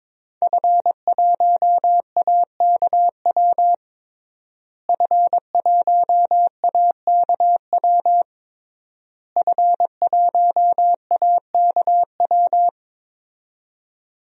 Morse code, heard as a keyed tone, about 22 wpm, 700 Hz. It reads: F1AKW F1AKW F1AKW
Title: DXCC-100 Call Signs - 5 Characters - Repeated 3x - 22wpm